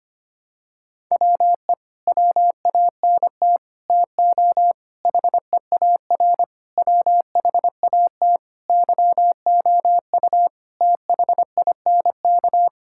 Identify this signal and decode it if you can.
Morse code, heard as a keyed tone, about 25 wpm, 700 Hz. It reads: WE WANT TO HEAR WHAT YOU THINK
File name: Sets of 9 Words - 25wpm